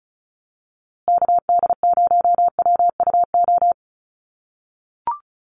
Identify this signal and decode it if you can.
Morse code, heard as a keyed tone, about 34 wpm, 700 Hz. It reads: KD0WUO
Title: USA Call Signs Repeated 3x 35wpm